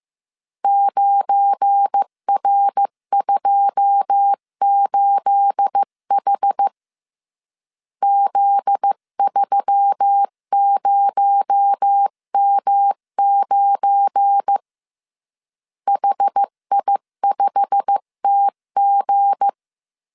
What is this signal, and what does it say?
Morse code, heard as a keyed tone, about 15 wpm, 800 Hz. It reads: 9R28H Z30M9 HI5TG
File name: CW / DET / 1 9 1500C S B 14wpm 3302